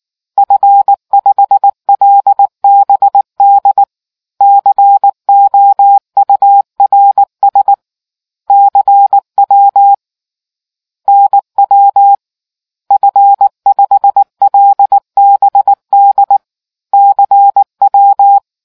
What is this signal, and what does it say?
Morse code, heard as a keyed tone, about 19 wpm, 800 Hz. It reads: F5LBD COURS CW NW F5LBD CW